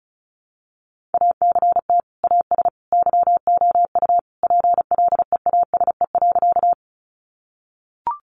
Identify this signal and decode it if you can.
Morse code, heard as a keyed tone, about 34 wpm, 700 Hz. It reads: ACT AS YOU PLEASE.
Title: Sentences from Top 1500 Words 35wpm